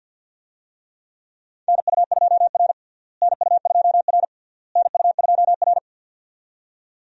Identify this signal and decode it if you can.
Morse code, heard as a keyed tone, about 51 wpm, 700 Hz. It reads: NU2R NU2R NU2R
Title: USA Call Signs - 4 Characters - Repeated 3x - 50wpm